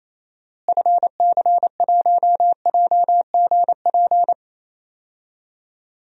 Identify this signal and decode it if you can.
Morse code, heard as a keyed tone, about 28 wpm, 700 Hz. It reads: FC1JGP
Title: DXCC-100 Call Signs - 6 Characters - Repeated 3x - 28wpm